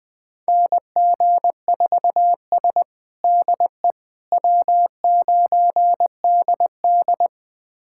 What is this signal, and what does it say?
Morse code, heard as a keyed tone, about 21 wpm, 700 Hz. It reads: NG4S DE W9DD
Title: USA Callsign DE Callsign - 4 Characters - 20wpm